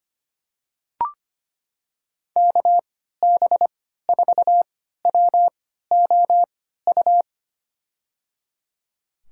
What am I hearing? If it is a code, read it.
Morse code, heard as a keyed tone, about 25 wpm, spaced as for 14 wpm, 700 Hz. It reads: KB4WOU
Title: USA Call Signs - 6 Characters - 3x Character Spacing - 25wpm